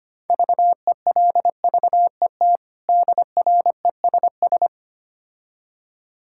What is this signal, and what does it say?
Morse code, heard as a keyed tone, about 25 wpm, 700 Hz. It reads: VELVET DRESS
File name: practice / 25wpm / Difficult Words with Hints - 25wpm